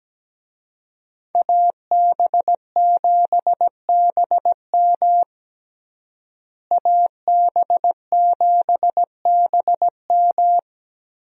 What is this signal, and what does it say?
Morse code, heard as a keyed tone, about 17 wpm, 700 Hz. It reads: AB7BM AB7BM
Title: USA Call Signs - 5 Characters - Repeated 3x - 17wpm